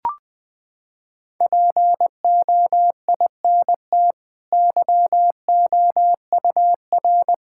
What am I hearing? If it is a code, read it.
Morse code, heard as a keyed tone, about 20 wpm, 700 Hz. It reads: POINT YOUR
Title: Sentences from Top 1600 Words 20wpm